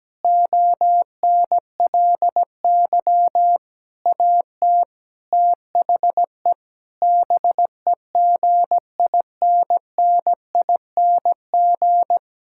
Morse code, heard as a keyed tone, about 17 wpm, 700 Hz. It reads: ONLY AT THE BEGINNING
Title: Sentences from Top 1000 Words - Review Part 5 - 17wpm